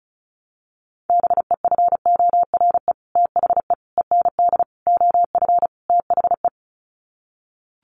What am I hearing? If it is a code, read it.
Morse code, heard as a keyed tone, about 34 wpm, 700 Hz. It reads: BEFORE THE END OF THE
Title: Sets of 5 Words Encore 35wpm